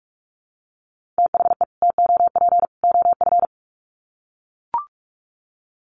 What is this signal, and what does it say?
Morse code, heard as a keyed tone, about 44 wpm, 700 Hz. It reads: THE TOP OF
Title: Sets of 3 Words 45wpm